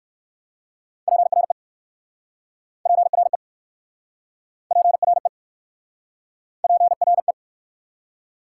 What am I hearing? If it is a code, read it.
Morse code, heard as a keyed tone, about 60 wpm, 690 Hz. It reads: PRE PRE PRE PRE